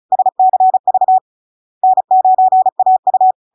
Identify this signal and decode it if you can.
Morse code, heard as a keyed tone, about 36 wpm, 750 Hz. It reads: SCV N9AU
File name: LARK-FD-2022-Log-in-CW-35wpm-750-Hz